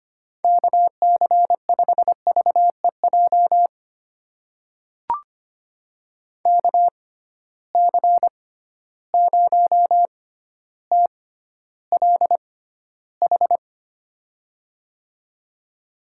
Morse code, heard as a keyed tone, about 25 wpm, spaced as for 13 wpm, 700 Hz. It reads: KC5VEJ KC0TLH
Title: USA Call Signs - 6 Characters - 6x Character Spacing - 25wpm